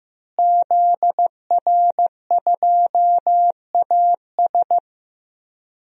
Morse code, heard as a keyed tone, about 15 wpm, 700 Hz. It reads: ZR2AS
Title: DXCC-100 Call Signs - 5 Characters - Repeated 3x - 15wpm